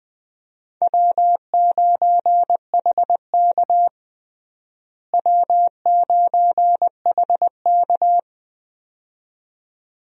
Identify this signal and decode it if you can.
Morse code, heard as a keyed tone, about 20 wpm, 700 Hz. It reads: W9HK W9HK
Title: USA Call Signs - 4 Characters - Repeated 2x - 20wpm